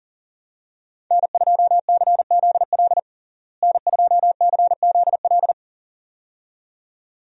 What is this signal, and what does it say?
Morse code, heard as a keyed tone, about 37 wpm, 700 Hz. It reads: N2CZL N2CZL
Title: USA Call Signs - 5 Characters - Repeated 2x - 40wpm